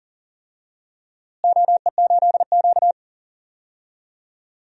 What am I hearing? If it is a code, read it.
Morse code, heard as a keyed tone, about 40 wpm, 700 Hz. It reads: OE8Q
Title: DXCC-100 Call Signs - 4 Characters - Repeated 3x - 40wpm